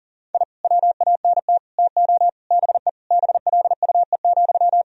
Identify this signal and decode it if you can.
Morse code, heard as a keyed tone, about 40 wpm, 700 Hz. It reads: I WANT TO BE BLUE,
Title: Sentences from Top 1000 Words - Review Part 3 - 40wpm